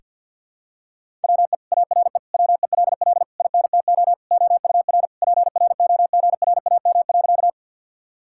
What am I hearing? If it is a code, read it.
Morse code, heard as a keyed tone, about 49 wpm, 700 Hz. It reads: WE ARE WELL INTO OUR PROGRAM.
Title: Sentences from Top 1000 Words - Review Part 5 - 50wpm